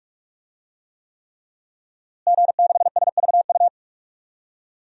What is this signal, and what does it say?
Morse code, heard as a keyed tone, about 46 wpm, 700 Hz. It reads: G6SVU
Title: DXCC-100 Call Signs - 5 Characters - Repeated 3x - 45wpm